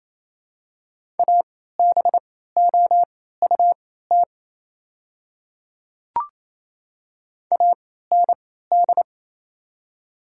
Morse code, heard as a keyed tone, about 27 wpm, spaced as for 16 wpm, 700 Hz. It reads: ABOUT AND
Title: 3x Character Spacing - Top 200 Words - 28wpm